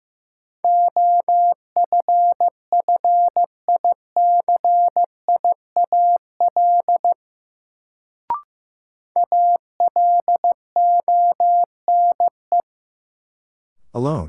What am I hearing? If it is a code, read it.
Morse code, heard as a keyed tone, about 15 wpm, 700 Hz. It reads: OFFICIAL ALONE
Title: Sentences from Top 1000 Words - New Word Drill - 15wpm